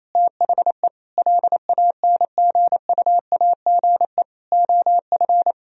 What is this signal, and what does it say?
Morse code, heard as a keyed tone, about 28 wpm, 700 Hz. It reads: THE LANGUAGE OF
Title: Sentences from Top 1000 Words - Review Part 3 - 28wpm